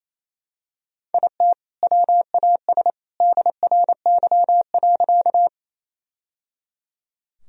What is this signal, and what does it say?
Morse code, heard as a keyed tone, about 28 wpm, 700 Hz. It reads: IT WAS DRY.
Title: Sentences from Top 1000 Words - Review Part 3 - 28wpm